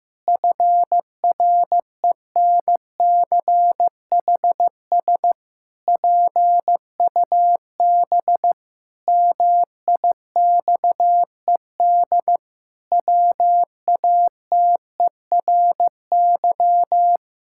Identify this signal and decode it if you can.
Morse code, heard as a keyed tone, about 15 wpm, 700 Hz. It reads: FRENCHS PUB MIXED WATERY
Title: Pangram Challenge v2 - 15wpm